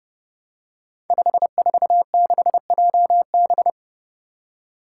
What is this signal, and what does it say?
Morse code, heard as a keyed tone, about 30 wpm, 700 Hz. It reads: 546JB
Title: Five Challenging Letter-Number Combinations 30wpm